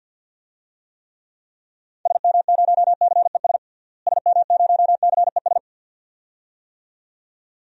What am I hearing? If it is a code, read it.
Morse code, heard as a keyed tone, about 46 wpm, 700 Hz. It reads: SM0CES SM0CES
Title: DXCC-100 Call Signs - 6 Characters - Repeated 2x - 50wpm